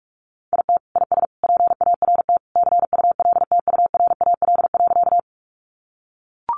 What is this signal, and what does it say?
Morse code, heard as a keyed tone, about 44 wpm, 700 Hz. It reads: IT IS PART CULTURAL.
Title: Sentences from Top 1500 Words 45wpm